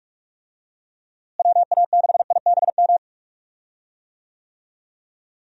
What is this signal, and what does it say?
Morse code, heard as a keyed tone, about 41 wpm, 700 Hz. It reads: WA6IBM
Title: USA Call Signs - 6 Characters - Repeated 2x - 45wpm